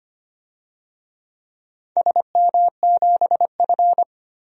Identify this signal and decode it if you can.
Morse code, heard as a keyed tone, about 25 wpm, 700 Hz. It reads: SM7F
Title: DXCC-100 Call Signs - 4 Characters - 25wpm